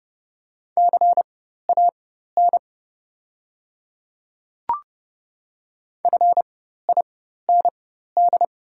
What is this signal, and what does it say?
Morse code, heard as a keyed tone, about 30 wpm, spaced as for 14 wpm, 690 Hz. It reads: CAN FIND
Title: 4x Character Spacing - Top 100 Words - 30wpm